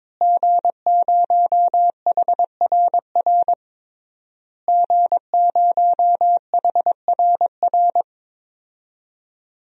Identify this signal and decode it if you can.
Morse code, heard as a keyed tone, about 21 wpm, 700 Hz. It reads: G0HRR G0HRR
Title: DXCC-100 Call Signs - 5 Characters - Repeated 3x - 22wpm